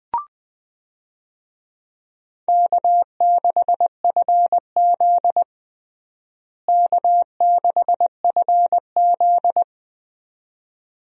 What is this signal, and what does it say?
Morse code, heard as a keyed tone, about 20 wpm, 700 Hz. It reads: K6FZ K6FZ
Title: USA Call Signs - 4 Characters - Repeated 2x - 20wpm